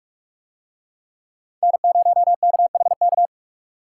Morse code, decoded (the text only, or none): N0KHK